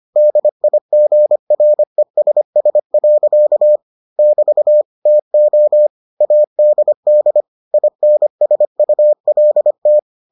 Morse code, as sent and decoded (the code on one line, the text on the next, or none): -.. .. --. .-. ... ... .-.-.- / -...- / - --- / .- -.. -.. / .. -. ... ..- .-.. -
DIGRESS. <BT> TO ADD INSULT